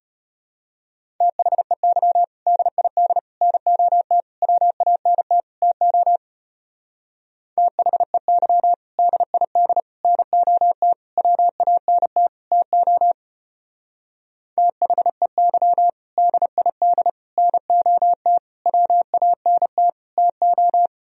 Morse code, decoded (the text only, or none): THEY DID NOT WANT TO THEY DID NOT WANT TO THEY DID NOT WANT TO